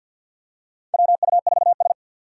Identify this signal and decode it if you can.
Morse code, heard as a keyed tone, about 47 wpm, 700 Hz. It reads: WU4S
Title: USA Call Signs - 4x Character Spacing - 50wpm